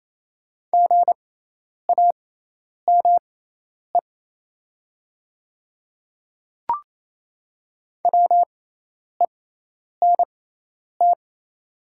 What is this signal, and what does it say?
Morse code, heard as a keyed tone, about 28 wpm, spaced as for 10 wpm, 700 Hz. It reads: GAME WENT
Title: 6x Character Spacing - Top 500 Words - 28wpm